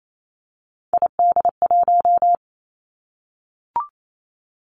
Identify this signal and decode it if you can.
Morse code, heard as a keyed tone, about 28 wpm, 700 Hz. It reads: ID1
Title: Three Challenging Letter-Number Combinations 28wpm